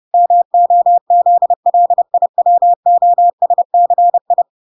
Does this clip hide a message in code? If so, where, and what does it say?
Yes: Morse code from 0.1 to 4.4 s, heard as a keyed tone, about 30 wpm, 700 Hz. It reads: MOZLIWOSCI